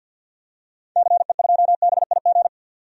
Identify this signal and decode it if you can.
Morse code, heard as a keyed tone, about 47 wpm, 700 Hz. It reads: CE2BIG
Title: DXCC-100 Call Signs - 6 Characters - 50wpm